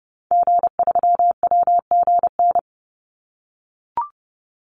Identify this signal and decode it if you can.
Morse code, heard as a keyed tone, about 30 wpm, 700 Hz. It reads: G3WGN
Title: Call Signs - 30wpm